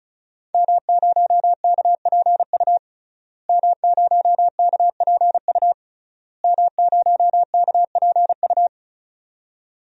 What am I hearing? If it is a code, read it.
Morse code, heard as a keyed tone, about 34 wpm, 700 Hz. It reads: M0KPU M0KPU M0KPU
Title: DXCC-100 Call Signs - 5 Characters - Repeated 3x - 35wpm